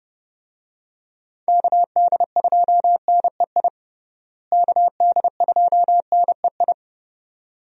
Morse code, heard as a keyed tone, about 30 wpm, 700 Hz. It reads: KD2NEI KD2NEI
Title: USA Call Signs - 6 Characters - Repeated 2x - 30wpm